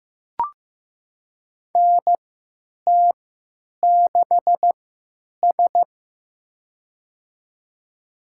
Morse code, heard as a keyed tone, about 15 wpm, spaced as for 9 wpm, 700 Hz. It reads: NT6S
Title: USA Call Signs - 4 Characters - 3x Character Spacing - 15wpm